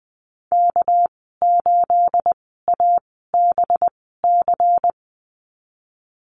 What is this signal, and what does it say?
Morse code, heard as a keyed tone, about 20 wpm, spaced as for 15 wpm, 700 Hz. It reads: K8ABC
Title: 238 Lesson 38g - Letter q - Call Signs 20wpm